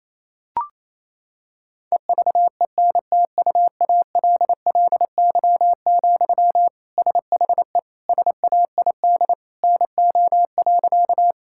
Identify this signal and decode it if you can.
Morse code, heard as a keyed tone, about 28 wpm, 700 Hz. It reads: EVENTUALLY, SHE SAID NO.
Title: Sentences from Top 1000 Words - Review Part 6 - 28wpm